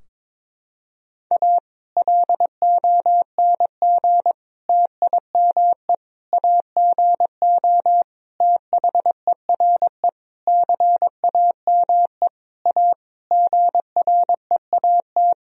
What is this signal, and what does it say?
Morse code, heard as a keyed tone, about 21 wpm, 700 Hz. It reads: A LONG TIME AGO THERE CAME A GREAT
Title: Sentences from Top 1000 Words - Review Part 3 - 22wpm